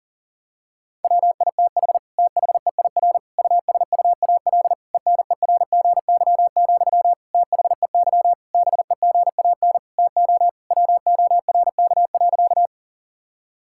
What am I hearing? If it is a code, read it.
Morse code, heard as a keyed tone, about 40 wpm, 700 Hz. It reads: WITH THEIR USUAL ENERGY, THEY BEGAN TO WORK.
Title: Sentences from Top 1000 Words - Review Part 6 - 40wpm